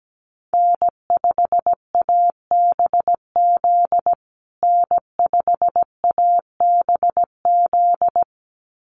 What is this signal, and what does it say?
Morse code, heard as a keyed tone, about 17 wpm, 700 Hz. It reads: N5ABZ N5ABZ
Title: USA Call Signs Repeated 2x 17wpm